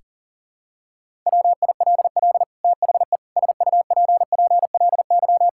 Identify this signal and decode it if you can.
Morse code, heard as a keyed tone, about 40 wpm, 700 Hz. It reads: WILL THE SUPPLY
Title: Sentences from Top 1000 Words - Review Part 5 - 40wpm